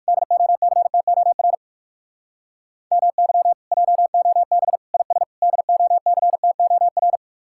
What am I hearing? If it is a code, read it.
Morse code, heard as a keyed tone, about 59 wpm, 700 Hz. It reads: DOCTOR MY JOB IS DOCTOR